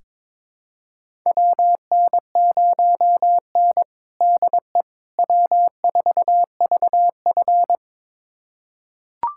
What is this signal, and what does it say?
Morse code, heard as a keyed tone, about 22 wpm, 700 Hz. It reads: WN0N DE W4VF
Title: USA Callsign DE Callsign - 4 Characters - 22wpm